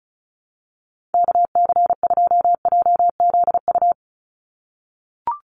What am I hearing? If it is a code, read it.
Morse code, heard as a keyed tone, about 34 wpm, 700 Hz. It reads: KC2JZU
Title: USA Call Signs 35wpm